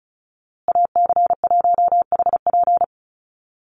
Morse code, heard as a keyed tone, about 34 wpm, 700 Hz. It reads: AC1HP
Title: USA Call Signs Repeated 2x 35wpm